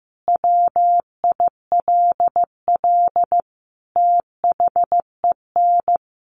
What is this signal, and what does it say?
Morse code, heard as a keyed tone, about 14 wpm, 700 Hz. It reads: WILL THEN
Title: Sentences from Top 1500 Words 15wpm